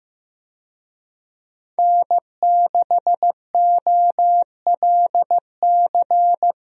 Morse code, heard as a keyed tone, about 15 wpm, 700 Hz. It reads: N6OLC
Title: USA Call Signs - 5 Characters - Repeated 2x - 15wpm